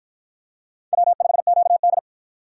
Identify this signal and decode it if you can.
Morse code, heard as a keyed tone, about 53 wpm, 700 Hz. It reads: W5QD